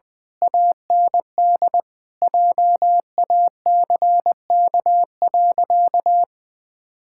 Morse code, heard as a keyed tone, about 21 wpm, 700 Hz. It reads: AND JACK.
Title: Pangram Challenge v2 - 20wpm